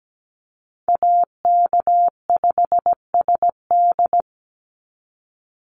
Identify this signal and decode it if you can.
Morse code, heard as a keyed tone, about 17 wpm, 700 Hz. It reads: AK5SD
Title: NA SOTA Chaser Callsigns 17wpm